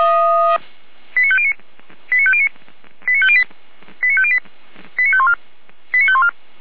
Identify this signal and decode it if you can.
Five-tone selective call (ZVEI), heard as quick runs of five tones, tones of about 80 ms. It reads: 89509 89509 89508 89508 89514 89514
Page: ZVEI